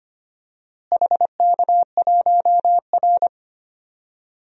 Morse code, heard as a keyed tone, about 25 wpm, 700 Hz. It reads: HK1R
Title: DXCC-100 Call Signs - 4 Characters - Repeated 3x - 25wpm